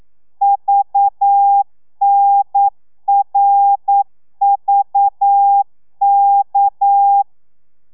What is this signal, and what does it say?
Morse code, heard as a keyed tone, about 9 wpm, 800 Hz. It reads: VNRVK
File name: telegrafia / i2tfj / Lez6 40